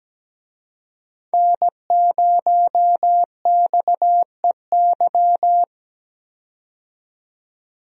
Morse code, heard as a keyed tone, about 17 wpm, 700 Hz. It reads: N0XEY
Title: USA Call Signs - 5 Characters - 17wpm